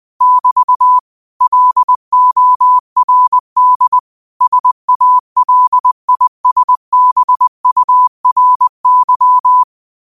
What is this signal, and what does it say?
Morse code, heard as a keyed tone, about 20 wpm, 1000 Hz. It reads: <BT> LORD SALISBURY